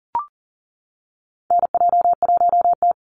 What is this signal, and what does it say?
Morse code, heard as a keyed tone, about 37 wpm, 700 Hz. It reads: NJ1T
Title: Call Signs Repeated 3x - 40wpm